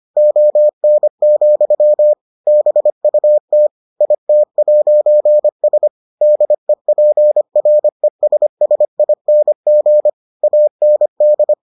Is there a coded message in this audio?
Yes: Morse code from 0.2 to 11.5 s, heard as a keyed tone, about 25 wpm, 600 Hz. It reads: ON, BUT IT'S DEPRESSING AND